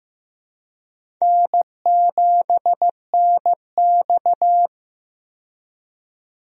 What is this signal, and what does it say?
Morse code, heard as a keyed tone, about 15 wpm, 700 Hz. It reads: N7NX
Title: USA Call Signs - 4 Characters - 15wpm